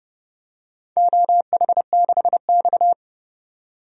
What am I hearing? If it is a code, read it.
Morse code, heard as a keyed tone, about 30 wpm, 700 Hz. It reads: OH6X
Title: DXCC-100 Call Signs - 4 Characters - 30wpm